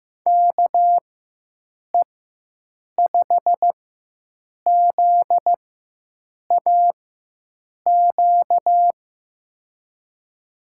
Morse code, heard as a keyed tone, about 15 wpm, spaced as for 7 wpm, 700 Hz. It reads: KE5ZAQ
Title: USA Call Signs - 6 Characters - 4x Character Spacing - 15wpm